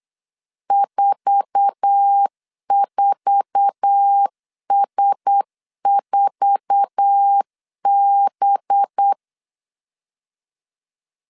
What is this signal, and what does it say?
Morse code, heard as a keyed tone, about 8 wpm, 800 Hz. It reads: 44S4B
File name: LEZIONI / 8 YJ45 / 8 YJ4 8wpm 3246